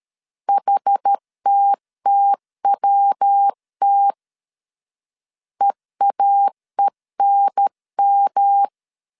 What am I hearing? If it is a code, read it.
Morse code, heard as a keyed tone, about 13 wpm, 800 Hz. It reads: HTTWT EAENM
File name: CW / DET / 1 3 1500C S A 12wpm 2905